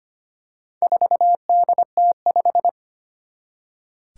Morse code, heard as a keyed tone, about 25 wpm, 700 Hz. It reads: 4DT5
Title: Four Challenging Letter-Number Combinations 25wpm